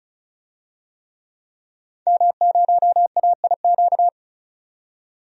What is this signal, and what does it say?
Morse code, heard as a keyed tone, about 36 wpm, 700 Hz. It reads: M0AIQ
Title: DXCC-100 Call Signs - 5 Characters - 35wpm